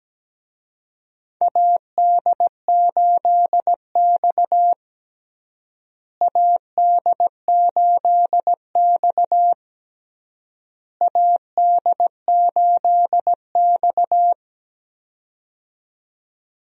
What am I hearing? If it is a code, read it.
Morse code, heard as a keyed tone, about 17 wpm, 700 Hz. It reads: AD8X AD8X AD8X
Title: USA Call Signs - 4 Characters - Repeated 3x - 17wpm